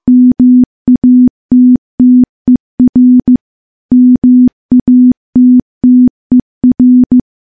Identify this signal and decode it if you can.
Morse code, heard as a keyed tone, about 15 wpm, 250 Hz. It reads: MATTER MATTER
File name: morse / morse-training-15WPM